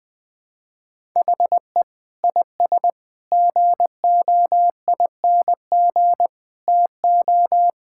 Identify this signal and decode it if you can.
Morse code, heard as a keyed tone, about 20 wpm, 700 Hz. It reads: HE IS GOING TO
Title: Sentences with 3-Word Phrases - 20wpm